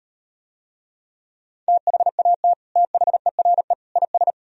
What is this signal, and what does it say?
Morse code, heard as a keyed tone, about 38 wpm, 700 Hz. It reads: THAT THERE IS